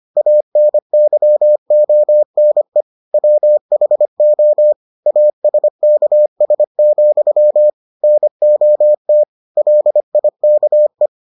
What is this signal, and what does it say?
Morse code, heard as a keyed tone, about 25 wpm, 600 Hz. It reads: ANYONE WHO ASKS, NOT LIKE